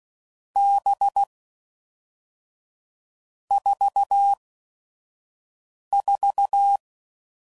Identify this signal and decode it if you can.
Morse code, heard as a keyed tone, about 16 wpm, spaced as for 5 wpm, 800 Hz. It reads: B 44
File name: Groups / Basic Letter Groups / VB46